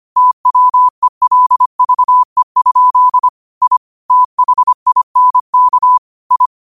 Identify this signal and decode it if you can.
Morse code, heard as a keyed tone, about 25 wpm, 1000 Hz. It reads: TWELVE? I THINK I
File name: qotdfiles / 25wpm